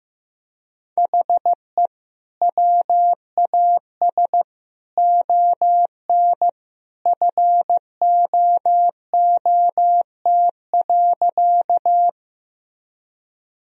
Morse code, heard as a keyed tone, about 15 wpm, 700 Hz. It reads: HE WAS ON FOOT.
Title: Sentences from Top 1000 Words - Review Part 3 - 15wpm